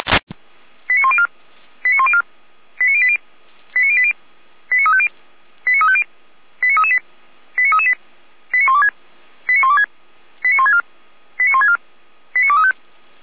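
Five-tone selective call (ZVEI), heard as quick runs of five tones, tones of about 70 ms. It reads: 89104 89104 89080 89080 89350 89350 89338 89338 89126 89126 89164 89164 89235
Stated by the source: ZVEI five-tone call received by SDR.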